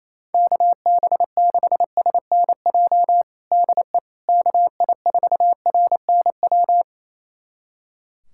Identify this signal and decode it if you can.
Morse code, heard as a keyed tone, about 28 wpm, 700 Hz. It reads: KB6SNJ DE KI4RNW
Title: USA Callsign DE Callsign - 6 Characters - 28wpm